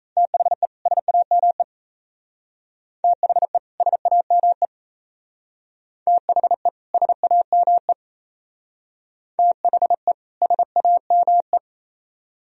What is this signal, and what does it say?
Morse code, heard as a keyed tone, about 35 wpm, 700 Hz. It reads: THE SAME THE SAME THE SAME THE SAME
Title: Speed-Racing - Sets of 2 Words 28wpm